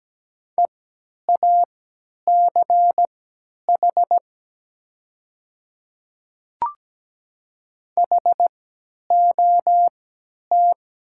Morse code, heard as a keyed tone, about 17 wpm, spaced as for 10 wpm, 700 Hz. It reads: EACH HOT
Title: 3x Character Spacing - Top 100 Words - 17wpm